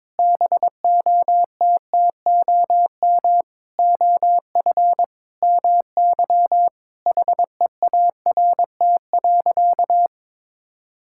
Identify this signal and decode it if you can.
Morse code, heard as a keyed tone, about 22 wpm, 700 Hz. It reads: BOTTOM OF MY HEART.
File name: practice / 22wpm / Sentences with 8-Word Phrases - 22wpm